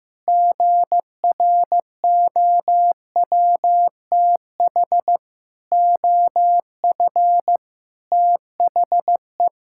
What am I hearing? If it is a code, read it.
Morse code, heard as a keyed tone, about 16 wpm, 700 Hz. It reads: GROWTH OF THE
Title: Sentences from Top 1000 Words - Review Part 5 - 15wpm